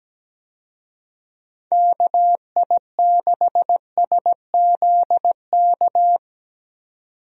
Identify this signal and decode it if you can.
Morse code, heard as a keyed tone, about 17 wpm, 700 Hz. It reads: KI6SZK
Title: USA Call Signs - 6 Characters - 17wpm